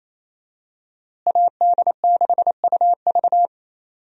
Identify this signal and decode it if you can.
Morse code, heard as a keyed tone, about 28 wpm, 700 Hz. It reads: AD6UV
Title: USA Call Signs - 5 Characters - Repeated 3x - 28wpm